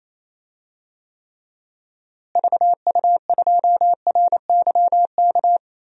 Morse code, heard as a keyed tone, about 28 wpm, 700 Hz. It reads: VU2RYK
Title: DXCC-100 Call Signs - 6 Characters - Repeated 3x - 28wpm